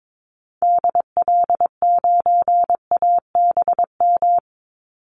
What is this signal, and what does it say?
Morse code, heard as a keyed tone, about 22 wpm, 700 Hz. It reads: DL9ABM
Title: Call Signs - 22wpm